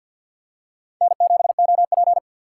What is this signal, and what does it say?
Morse code, heard as a keyed tone, about 53 wpm, 700 Hz. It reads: N7OP
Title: USA Call Signs - 4x Character Spacing - 50wpm